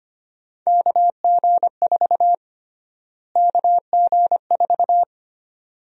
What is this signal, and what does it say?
Morse code, heard as a keyed tone, about 25 wpm, 700 Hz. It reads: KG4 KG4
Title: USA Call Sign Prefixes - Repeated 2x - 25wpm